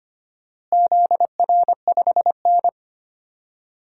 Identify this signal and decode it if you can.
Morse code, heard as a keyed tone, about 25 wpm, 700 Hz. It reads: ZR5N
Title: DXCC-100 Call Signs - 4 Characters - Repeated 3x - 25wpm